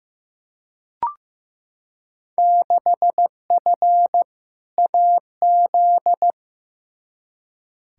Morse code, heard as a keyed tone, about 15 wpm, 700 Hz. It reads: 6F AZ
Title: Field Day Exchanges Encore - 15wpm